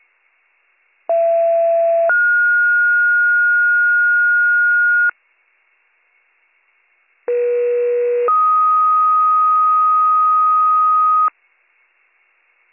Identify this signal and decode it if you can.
Two-tone page, heard as a one-second tone, then a three-second tone, about 670 then 1430 Hz, about 480 then 1250 Hz. It is heard from 1.1 to 11.3 s.